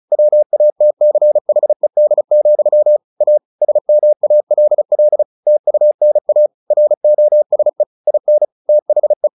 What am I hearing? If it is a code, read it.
Morse code, heard as a keyed tone, about 34 wpm, 600 Hz. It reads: WATCHED, A SMALL TUNA ROSE IN THE
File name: the old man and the sea / the old man and the sea 1 35wpm